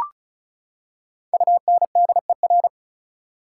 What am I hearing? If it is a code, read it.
Morse code, heard as a keyed tone, about 34 wpm, 700 Hz. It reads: UNDER
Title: Prefixes and Suffixes 35wpm